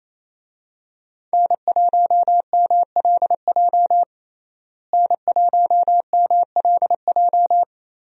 Morse code, heard as a keyed tone, about 27 wpm, 700 Hz. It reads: N1MLJ N1MLJ
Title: USA Call Signs - 5 Characters - Repeated 3x - 28wpm